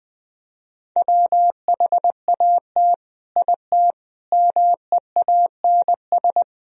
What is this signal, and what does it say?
Morse code, heard as a keyed tone, about 20 wpm, 700 Hz. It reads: WHAT IT MEANS
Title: Speed-Racing - Sets of 5 Words 20wpm